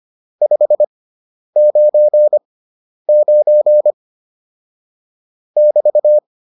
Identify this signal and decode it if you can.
Morse code, heard as a keyed tone, about 25 wpm, spaced as for 10 wpm, 600 Hz. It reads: 599 <BT>
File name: QSO108 10